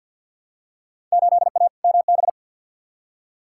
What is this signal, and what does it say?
Morse code, heard as a keyed tone, about 52 wpm, 700 Hz. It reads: 8A MB
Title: Field Day Exchanges Encore - 50wpm